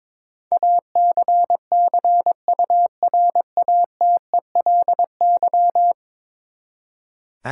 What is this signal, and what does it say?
Morse code, heard as a keyed tone, about 22 wpm, 700 Hz. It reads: ACCURATELY
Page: morse-code-ninja-practice-22wpm